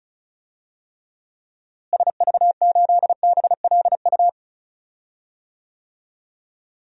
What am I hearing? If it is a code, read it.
Morse code, heard as a keyed tone, about 36 wpm, 700 Hz. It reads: SV8BLU
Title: DXCC-100 Call Signs - 6 Characters - Repeated 2x - 35wpm